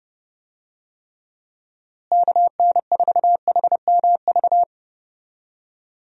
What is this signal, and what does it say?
Morse code, heard as a keyed tone, about 29 wpm, 700 Hz. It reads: KN4HMV
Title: USA Call Signs - 6 Characters - Repeated 3x - 30wpm